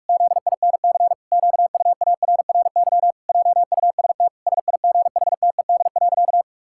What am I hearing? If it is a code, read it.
Morse code, heard as a keyed tone, about 46 wpm, 700 Hz. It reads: ZINC QUARRY JUST SIGHTED.
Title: Pangram Challenge v2 - 45wpm